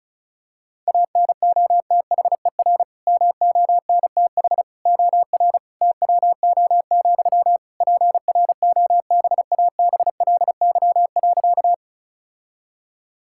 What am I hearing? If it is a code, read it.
Morse code, heard as a keyed tone, about 34 wpm, 700 Hz. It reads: ANOTHER MONTH OR TWO, PROBABLY.
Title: Sentences from Top 1000 Words - Review Part 5 - 35wpm